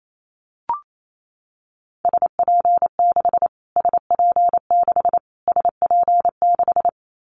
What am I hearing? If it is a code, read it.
Morse code, heard as a keyed tone, about 28 wpm, 700 Hz. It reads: SP6 SP6 SP6
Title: Callsign Prefixes for Top 25 DXCC - Repeated 3x - 28wpm